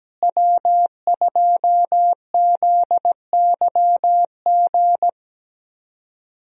Morse code, heard as a keyed tone, about 17 wpm, 700 Hz. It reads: W2ZYG
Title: USA Call Signs - 5 Characters - Repeated 2x - 17wpm